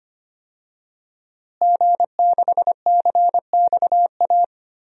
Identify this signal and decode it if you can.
Morse code, heard as a keyed tone, about 25 wpm, 700 Hz. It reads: G6CXA
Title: DXCC-100 Call Signs - 5 Characters - Repeated 2x - 25wpm